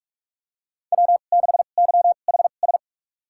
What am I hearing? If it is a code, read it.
Morse code, heard as a keyed tone, about 46 wpm, spaced as for 32 wpm, 700 Hz. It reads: W6YHS
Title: USA Call Signs - 5 Characters - 2x Character Spacing - 45wpm